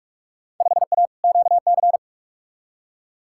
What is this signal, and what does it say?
Morse code, heard as a keyed tone, about 44 wpm, 700 Hz. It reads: 5A QC
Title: Field Day Exchanges Encore - 45wpm